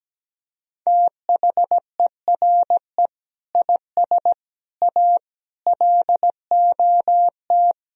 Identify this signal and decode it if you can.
Morse code, heard as a keyed tone, about 17 wpm, 700 Hz. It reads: THERE IS A LOT
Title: Speed-Racing - Sets of 5 Words 17wpm